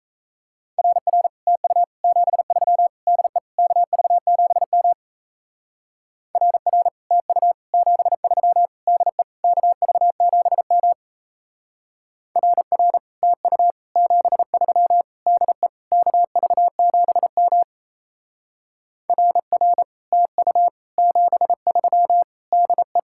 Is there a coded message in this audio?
Yes: Morse code from 0.8 to 23.0 s, heard as a keyed tone, about 35 wpm, 700 Hz. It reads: RR TU 73 DE KV7M RR TU 73 DE KV7M RR TU 73 DE KV7M RR TU 73 DE